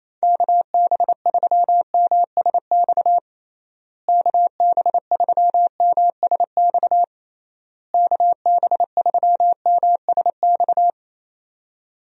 Morse code, heard as a keyed tone, about 28 wpm, 700 Hz. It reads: KB3MSX KB3MSX KB3MSX
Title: USA Call Signs - 6 Characters - Repeated 3x - 28wpm